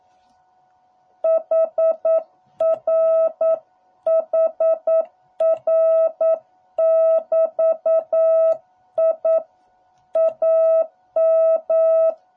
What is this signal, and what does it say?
Morse code, heard as a keyed tone, about 9 wpm, 650 Hz. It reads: HR HR <BT> I AM